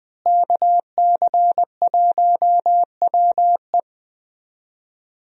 Morse code, heard as a keyed tone, about 21 wpm, 700 Hz. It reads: KC1WE
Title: USA Call Signs - 5 Characters - Repeated 2x - 20wpm